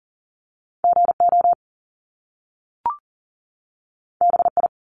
Morse code, heard as a keyed tone, about 40 wpm, 700 Hz. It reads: GO BI
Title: Top Two Letter Combinations in Words 40wpm